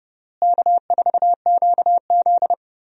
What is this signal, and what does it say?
Morse code, heard as a keyed tone, about 30 wpm, 700 Hz. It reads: K4QZ
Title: USA Call Signs - 4 Characters - Repeated 2x - 30wpm